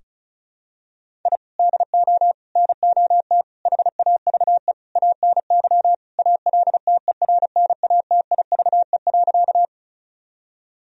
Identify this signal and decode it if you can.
Morse code, heard as a keyed tone, about 34 wpm, 700 Hz. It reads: I DO NOT HAVE ANY ALTERNATIVE.
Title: Sentences from Top 1000 Words - Review Part 4 - 35wpm